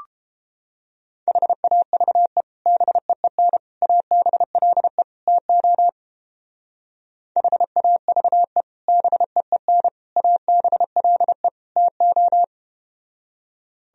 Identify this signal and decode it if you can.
Morse code, heard as a keyed tone, about 30 wpm, 700 Hz. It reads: HAVE BEEN ABLE TO HAVE BEEN ABLE TO